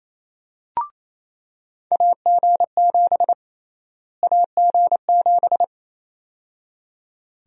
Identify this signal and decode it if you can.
Morse code, heard as a keyed tone, about 27 wpm, 700 Hz. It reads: AG7 AG7
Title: USA Call Sign Prefixes - Repeated 2x - 28wpm